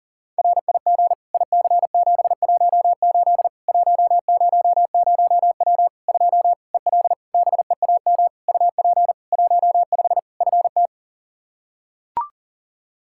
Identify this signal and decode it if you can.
Morse code, heard as a keyed tone, about 37 wpm, 700 Hz. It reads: RIG IC718 100W 2 EL BEAM UP 15 FT
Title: Practice QSO - Exchange 2 - Info - 40wpm